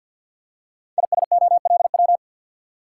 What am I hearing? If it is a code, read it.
Morse code, heard as a keyed tone, about 52 wpm, 700 Hz. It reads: ISOLW
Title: Code Groups - 5 Letter:Numbers - 50wpm